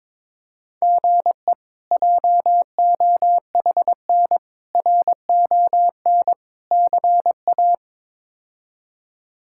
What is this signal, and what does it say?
Morse code, heard as a keyed tone, about 22 wpm, 700 Hz. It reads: GE JOHN RON CA
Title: K1USN SST QSO Practice - 22wpm